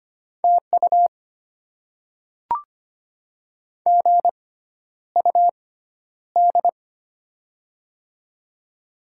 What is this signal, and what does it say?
Morse code, heard as a keyed tone, about 25 wpm, spaced as for 11 wpm, 700 Hz. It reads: TU GUD
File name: Top 20 QSO Elements - 6x Character Spacing - 25wpm